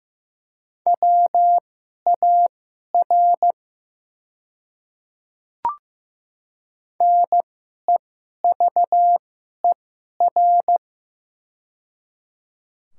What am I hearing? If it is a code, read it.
Morse code, heard as a keyed tone, about 15 wpm, spaced as for 11 wpm, 690 Hz. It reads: WAR NEVER
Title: 2x Character Spacing - Top 500 Words - 15wpm